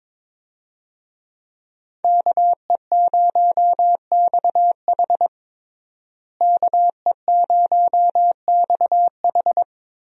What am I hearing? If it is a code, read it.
Morse code, heard as a keyed tone, about 22 wpm, 700 Hz. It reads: KE0XH KE0XH
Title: USA Call Signs - 5 Characters - Repeated 2x - 22wpm